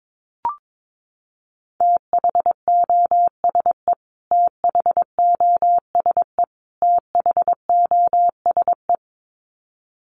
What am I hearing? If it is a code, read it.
Morse code, heard as a keyed tone, about 22 wpm, 700 Hz. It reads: THOSE THOSE THOSE
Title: Top 300 Words Repeated 22wpm